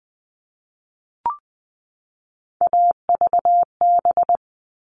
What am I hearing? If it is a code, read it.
Morse code, heard as a keyed tone, about 20 wpm, 700 Hz. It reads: AVB
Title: All Cut Numbers - 3-Digit 20wpm